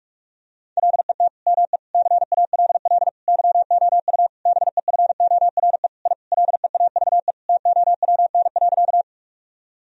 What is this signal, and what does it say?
Morse code, heard as a keyed tone, about 46 wpm, 700 Hz. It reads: LET ME CALL YOU BEFORE I LEAVE TOWN.
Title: Sentences from Top 1000 Words - Review Part 2 - 45wpm